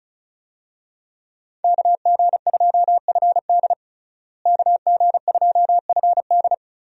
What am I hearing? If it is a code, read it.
Morse code, heard as a keyed tone, about 36 wpm, 700 Hz. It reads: KG2FD KG2FD
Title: USA Call Signs - 5 Characters - Repeated 2x - 35wpm